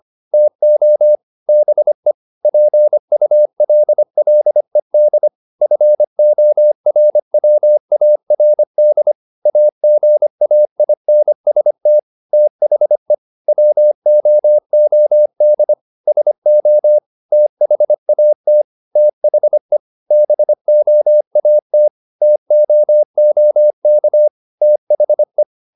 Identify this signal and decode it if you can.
Morse code, heard as a keyed tone, about 25 wpm, 600 Hz. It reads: TO BE PULLED FORWARD AGAINST THE WOOD SO THAT THE BOAT TOOK THE